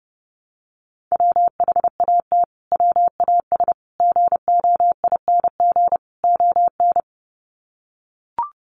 Morse code, heard as a keyed tone, about 30 wpm, 700 Hz. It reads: WHAT WAS GOING ON
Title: Sets of 4 Words 30wpm